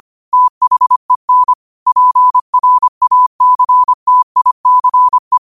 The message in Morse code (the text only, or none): THEN PRACTICE